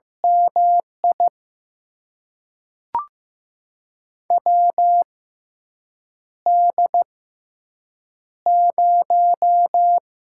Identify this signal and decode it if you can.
Morse code, heard as a keyed tone, about 15 wpm, spaced as for 7 wpm, 700 Hz. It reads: MI WD0
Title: USA Call Signs - 6 Characters - 6x Character Spacing - 15wpm